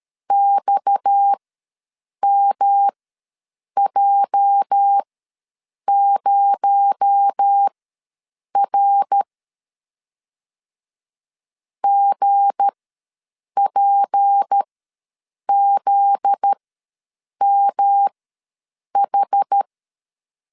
Morse code, heard as a keyed tone, about 13 wpm, spaced as for 7 wpm, 800 Hz. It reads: XMJ0R GPZMH
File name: LEZIONI / 8 YJ45 / 8 YJ4 ws 12wpm 3249